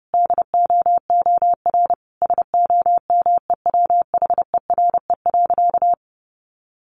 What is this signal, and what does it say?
Morse code, heard as a keyed tone, about 28 wpm, 700 Hz. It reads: DOOR SOMEWHERE.
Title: Sentences from Top 1400 Words 30wpm